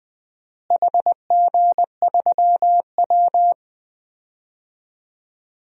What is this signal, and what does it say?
Morse code, heard as a keyed tone, about 20 wpm, 700 Hz. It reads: HG3W
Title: DXCC-100 Call Signs - 4 Characters - Repeated 3x - 20wpm